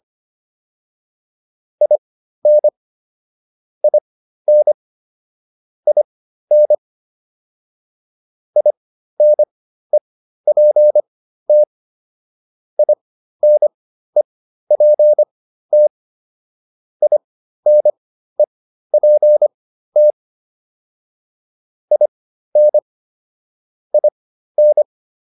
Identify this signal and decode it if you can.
Morse code, heard as a keyed tone, about 25 wpm, spaced as for 13 wpm, 600 Hz. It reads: IN IN IN INEPT INEPT INEPT IN IN